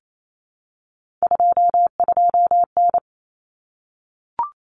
Morse code, heard as a keyed tone, about 28 wpm, 700 Hz. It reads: 22N
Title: Common Cut Numbers - 3-Digit 28wpm